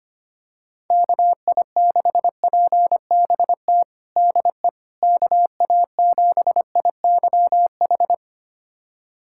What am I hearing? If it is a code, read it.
Morse code, heard as a keyed tone, about 25 wpm, 700 Hz. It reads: KI6PBT DE KA7IYH
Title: USA Callsign DE Callsign - 6 Characters - 25wpm